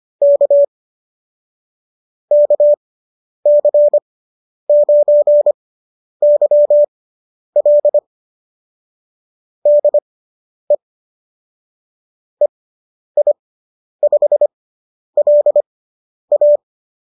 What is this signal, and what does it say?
Morse code, heard as a keyed tone, about 25 wpm, spaced as for 10 wpm, 570 Hz. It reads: K KC9YL DE EI5LA